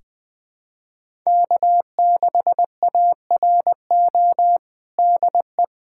Morse code, heard as a keyed tone, about 20 wpm, 700 Hz. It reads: K6ARO DE
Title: USA Callsign DE Callsign - 5 Characters - 20wpm